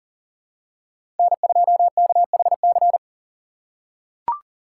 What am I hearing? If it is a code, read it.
Morse code, heard as a keyed tone, about 37 wpm, 700 Hz. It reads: N2KHC
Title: USA Call Signs - 6x Character Spacing - 40wpm